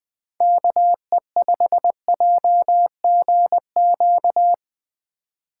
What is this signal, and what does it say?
Morse code, heard as a keyed tone, about 22 wpm, 700 Hz. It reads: KE5JGQ
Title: Speed Racing - USA 2x3 Call Signs 20wpm